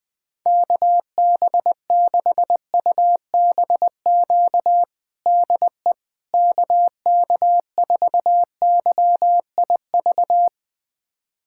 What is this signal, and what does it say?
Morse code, heard as a keyed tone, about 20 wpm, 700 Hz. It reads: KB6UBQ DE KK4YIV
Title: USA Callsign DE Callsign - 6 Characters - 20wpm